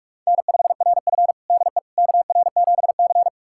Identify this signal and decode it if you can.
Morse code, heard as a keyed tone, about 43 wpm, 700 Hz. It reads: N5RF DE KR7C
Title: USA Callsign DE Callsign - 4 Characters - 45wpm